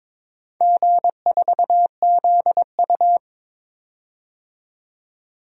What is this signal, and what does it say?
Morse code, heard as a keyed tone, about 22 wpm, 700 Hz. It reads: G4ZU
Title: DXCC-100 Call Signs - 4 Characters - 22wpm